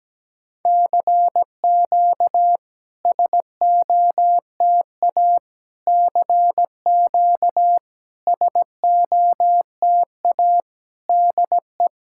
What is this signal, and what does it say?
Morse code, heard as a keyed tone, about 17 wpm, 700 Hz. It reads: CQ SOTA CQ SOTA DE
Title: SOTA QSO Practice 17wpm